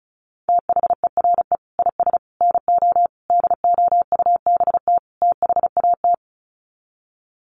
Morse code, heard as a keyed tone, about 34 wpm, 700 Hz. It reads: THERE IS NO DOUBT THAT
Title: Sets of 5 Words 35wpm